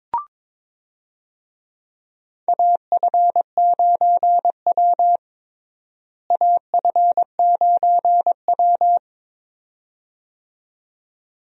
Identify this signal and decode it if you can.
Morse code, heard as a keyed tone, about 22 wpm, 700 Hz. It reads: AF9W AF9W
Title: USA Call Signs - 4 Characters - Repeated 2x - 22wpm